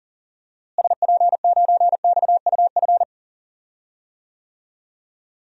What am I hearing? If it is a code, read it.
Morse code, heard as a keyed tone, about 40 wpm, 700 Hz. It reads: SP9XUF
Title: DXCC-100 Call Signs - 6 Characters - Repeated 2x - 40wpm